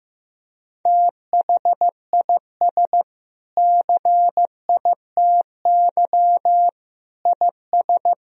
Morse code, heard as a keyed tone, about 15 wpm, 700 Hz. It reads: THIS CITY IS
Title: Sentences from Top 1000 Words - Review Part 4 - 15wpm